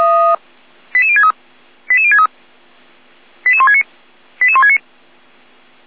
Five-tone selective call (ZVEI), heard as quick runs of five tones, tones of about 70 ms. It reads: 88063 88063 88169 88169